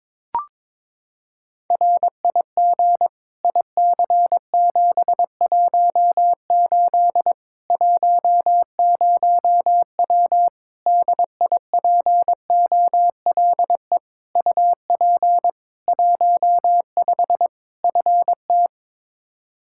Morse code, heard as a keyed tone, about 22 wpm, 700 Hz. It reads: RIG IC718 10W DIPOLE UP 15 FT
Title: Practice QSO - Exchange 2 - Info - 22wpm